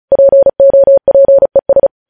Morse code, heard as a keyed tone, about 36 wpm, 550 Hz. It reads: POPES
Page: News Headlines in Morse Code at 35 WPM